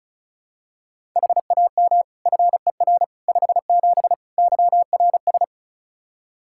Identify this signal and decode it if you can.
Morse code, heard as a keyed tone, about 36 wpm, 700 Hz. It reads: HAM FER 57 YRS
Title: K1USN SST QSO Practice 35wpm